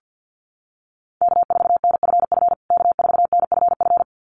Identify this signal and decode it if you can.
Morse code, heard as a keyed tone, about 50 wpm, 700 Hz. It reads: K4NFF K4NFF
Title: USA Call Signs Repeated 2x 50wpm